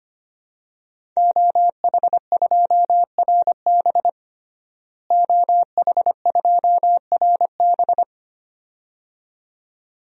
Morse code, heard as a keyed tone, about 25 wpm, 700 Hz. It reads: OH2RB OH2RB
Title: DXCC-100 Call Signs - 5 Characters - Repeated 2x - 25wpm